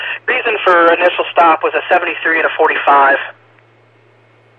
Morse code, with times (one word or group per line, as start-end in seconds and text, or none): none